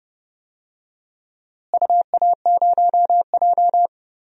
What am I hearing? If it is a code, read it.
Morse code, heard as a keyed tone, about 30 wpm, 700 Hz. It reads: UA0J
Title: DXCC-100 Call Signs - 4 Characters - Repeated 3x - 30wpm